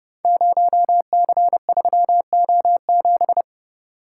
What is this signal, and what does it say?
Morse code, heard as a keyed tone, about 31 wpm, 700 Hz. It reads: 0C3O7
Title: Code Groups - 5 Letter:Numbers - 30wpm